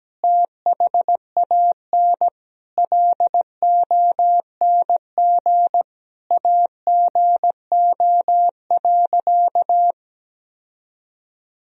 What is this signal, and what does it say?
Morse code, heard as a keyed tone, about 17 wpm, 700 Hz. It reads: THAN LONG AGO.
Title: Sentences from Top 1000 Words - Review Part 3 - 17wpm